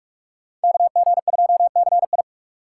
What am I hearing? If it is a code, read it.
Morse code, heard as a keyed tone, about 56 wpm, 700 Hz. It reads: KG2CI
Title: Speed Racing - USA Call Signs 45wpm